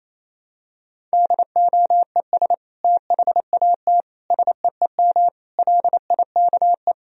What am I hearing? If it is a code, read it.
Morse code, heard as a keyed tone, about 27 wpm, 700 Hz. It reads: DOES THAT SEEM LIKE
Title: Sentences from Top 1000 Words - Review Part 1 - 28wpm